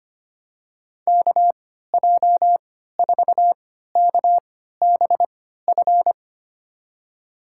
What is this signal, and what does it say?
Morse code, heard as a keyed tone, about 25 wpm, spaced as for 14 wpm, 700 Hz. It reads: KJ4KBF
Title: USA Call Signs - 6 Characters - 3x Character Spacing - 25wpm